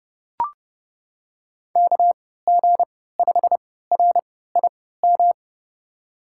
Morse code, heard as a keyed tone, about 29 wpm, spaced as for 17 wpm, 700 Hz. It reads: KG5RIM
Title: USA Call Signs - 6 Characters - 3x Character Spacing - 30wpm